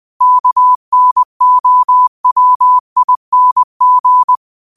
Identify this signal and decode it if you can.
Morse code, heard as a keyed tone, about 20 wpm, 1000 Hz. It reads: KNOWING
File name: qotdfiles / 20wpm